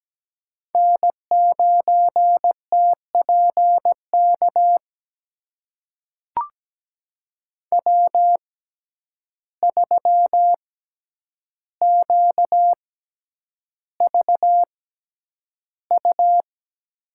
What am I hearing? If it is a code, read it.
Morse code, heard as a keyed tone, about 17 wpm, spaced as for 9 wpm, 700 Hz. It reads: N9TPK W3QVU
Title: USA Call Signs - 5 Characters - 6x Character Spacing - 17wpm